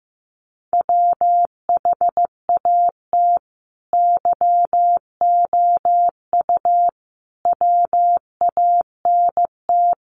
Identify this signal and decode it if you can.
Morse code, heard as a keyed tone, about 15 wpm, 700 Hz. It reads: WHAT YOU WANT
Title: Sets of 5 Words Encore 15wpm